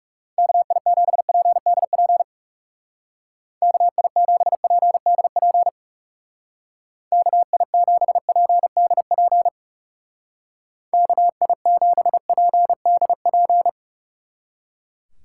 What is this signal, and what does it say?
Morse code, heard as a keyed tone, about 39 wpm, 700 Hz. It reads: KI7PDP KI7PDP KI7PDP KI7PDP